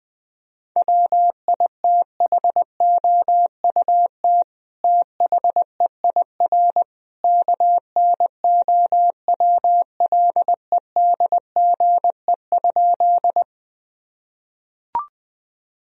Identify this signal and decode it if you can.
Morse code, heard as a keyed tone, about 20 wpm, 700 Hz. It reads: WITHOUT THEIR KNOWLEDGE?
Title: Sentences from Top 1000 Words - Review Part 3 - 20wpm